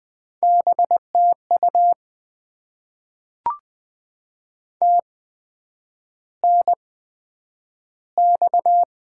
Morse code, heard as a keyed tone, about 20 wpm, spaced as for 9 wpm, 700 Hz. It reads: BTU TNX
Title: Top 20 QSO Elements - 8x Character Spacing - 20wpm